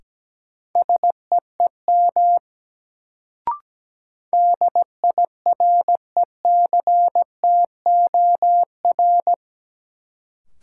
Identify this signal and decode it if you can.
Morse code, heard as a keyed tone, about 17 wpm, 700 Hz. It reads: SEEM DIRECTOR
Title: Sets of 3 Words - New Word Drill - 17wpm